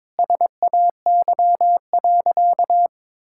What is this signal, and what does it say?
Morse code, heard as a keyed tone, about 22 wpm, 700 Hz. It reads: SAY.
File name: practice / 22wpm / Sentences with 4-Word Phrases - 22wpm